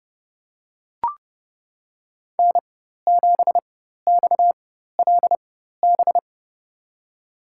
Morse code, heard as a keyed tone, about 31 wpm, spaced as for 14 wpm, 700 Hz. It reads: N7XLB
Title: USA Call Signs - 4x Character Spacing - 30wpm